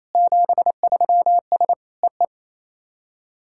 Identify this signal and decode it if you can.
Morse code, heard as a keyed tone, about 28 wpm, 700 Hz. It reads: 73S EE
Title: SOTA QSO Practice 28wpm